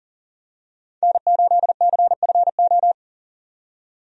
Morse code, heard as a keyed tone, about 40 wpm, 700 Hz. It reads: N8CFO
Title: USA Call Signs - 5 Characters - 40wpm